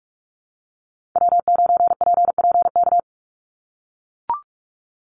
Morse code, heard as a keyed tone, about 44 wpm, 700 Hz. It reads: W9PPK
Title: USA Call Signs 45wpm